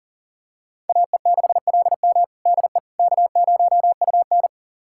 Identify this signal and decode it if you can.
Morse code, heard as a keyed tone, about 40 wpm, 700 Hz. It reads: AE6LM DE K0UN
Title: USA Callsign DE Callsign 40wpm